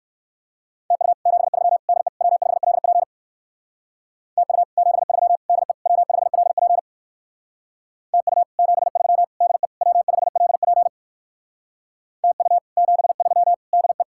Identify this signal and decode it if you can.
Morse code, heard as a keyed tone, about 62 wpm, 700 Hz. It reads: TU 73 DE W5LP TU 73 DE W5LP TU 73 DE W5LP TU 73 DE